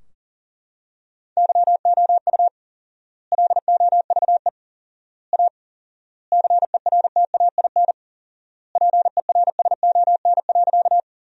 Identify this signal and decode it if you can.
Morse code, heard as a keyed tone, about 40 wpm, 700 Hz. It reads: YOU LOVE A CERTAIN PERSON.
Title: 4x Word-Spacing - Sentences from Top 500 Words - 40wpm